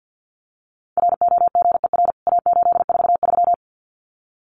Morse code, heard as a keyed tone, about 50 wpm, 700 Hz. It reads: ROGER A843
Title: CWT Exchange 50wpm